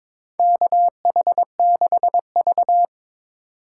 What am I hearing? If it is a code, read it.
Morse code, heard as a keyed tone, about 22 wpm, 700 Hz. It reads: KH6V
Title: USA Call Signs - 4 Characters - 4x Character Spacing - 22wpm